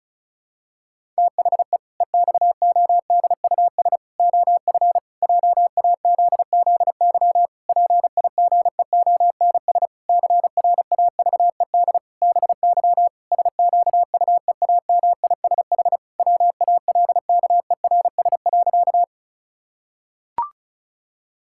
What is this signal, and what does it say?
Morse code, heard as a keyed tone, about 34 wpm, 700 Hz. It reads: THE EXODUS OF JAZZY PIGEONS CRAVED BY SQUEAMISH WALKERS.
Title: Pangram Challenge v2 - 35wpm